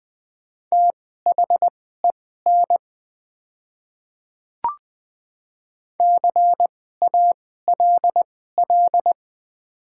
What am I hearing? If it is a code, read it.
Morse code, heard as a keyed tone, about 20 wpm, spaced as for 14 wpm, 700 Hz. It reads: THEN CALL
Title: 2x Character Spacing - Top 500 Words - 20wpm